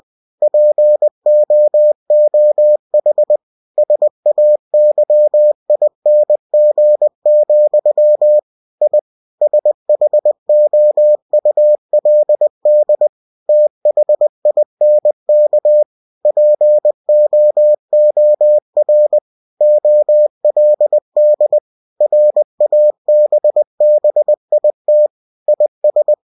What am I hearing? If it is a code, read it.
Morse code, heard as a keyed tone, about 20 wpm, 600 Hz. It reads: POOH SAYING, I SHOULD THINK POOR OLD RABBIT IS
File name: cw-books / winnie-the-pooh / winnie-20wpm-page-0091